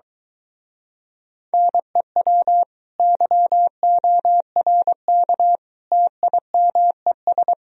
Morse code, heard as a keyed tone, about 23 wpm, 700 Hz. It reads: NEW YORK TIMES